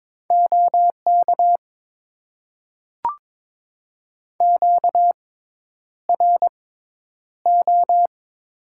Morse code, heard as a keyed tone, about 23 wpm, spaced as for 10 wpm, 700 Hz. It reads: OK QRO
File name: Top 90 QSO Elements - 6x Character Spacing - 22wpm